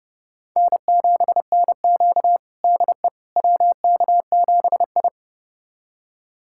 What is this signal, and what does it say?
Morse code, heard as a keyed tone, about 30 wpm, 700 Hz. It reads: N7NQ DE WK7I
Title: USA Callsign DE Callsign - 4 Characters - 30wpm